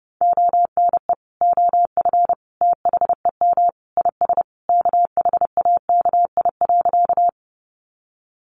Morse code, heard as a keyed tone, about 30 wpm, 700 Hz. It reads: ONE OF THEM IS KHAKI.
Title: Additional Easy Phrases 30wpm